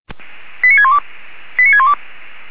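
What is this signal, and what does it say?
Five-tone selective call (ZVEI), heard as quick runs of five tones, tones of about 70 ms. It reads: 89612 89612
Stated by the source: Five-tone ZVEI selective call received by SDR.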